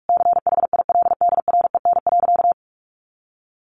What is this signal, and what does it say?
Morse code, heard as a keyed tone, about 44 wpm, 700 Hz. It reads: CHILDREN.
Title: Sentences from Top 300 Words 45wpm